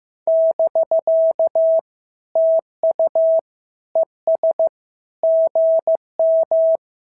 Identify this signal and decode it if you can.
Morse code, heard as a keyed tone, about 14 wpm, 650 Hz. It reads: <BK> TU ES GM
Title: SOTA QSO Practice 15wpm